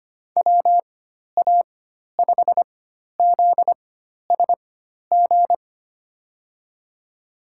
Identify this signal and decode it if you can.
Morse code, heard as a keyed tone, about 25 wpm, spaced as for 12 wpm, 700 Hz. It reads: WA5ZSG
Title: USA Call Signs - 6 Characters - 4x Character Spacing - 25wpm